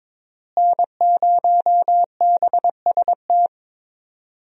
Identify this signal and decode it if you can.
Morse code, heard as a keyed tone, about 22 wpm, 700 Hz. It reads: N0BST
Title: USA Call Signs - 5 Characters - Repeated 2x - 22wpm